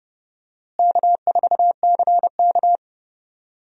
Morse code, heard as a keyed tone, about 29 wpm, 700 Hz. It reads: K4CK
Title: USA Call Signs - 4 Characters - Repeated 2x - 30wpm